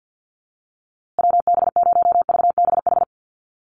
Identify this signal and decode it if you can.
Morse code, heard as a keyed tone, about 50 wpm, 700 Hz. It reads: WB0VBH
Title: USA Call Signs Repeated 3x 50wpm